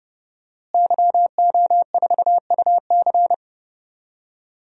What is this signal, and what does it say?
Morse code, heard as a keyed tone, about 30 wpm, 700 Hz. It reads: YO4UC
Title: DXCC-100 Call Signs - 5 Characters - Repeated 2x - 30wpm